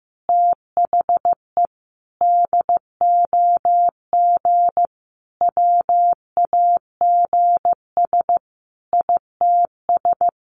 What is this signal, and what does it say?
Morse code, heard as a keyed tone, about 15 wpm, 700 Hz. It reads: THE DOG WAGS ITS
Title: Easy Phrases 15wpm